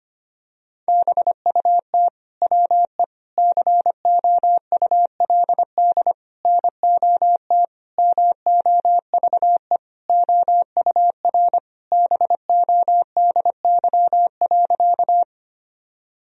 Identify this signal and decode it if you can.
Morse code, heard as a keyed tone, about 25 wpm, 700 Hz. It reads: BUT WE COULD NOT MOVE OUR BODY.
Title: Sentences from Top 1000 Words - Review Part 2 - 25wpm